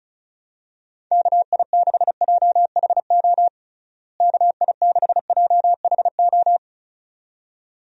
Morse code, heard as a keyed tone, about 34 wpm, 700 Hz. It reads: KI6JHO KI6JHO
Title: USA Call Signs - 6 Characters - Repeated 2x - 35wpm